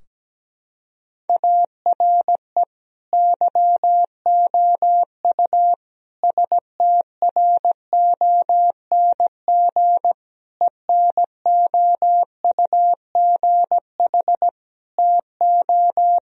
Morse code, heard as a keyed tone, about 17 wpm, 700 Hz. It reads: ARE YOU STRONG ENOUGH TO
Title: Sentences from Top 1000 Words - Review Part 3 - 17wpm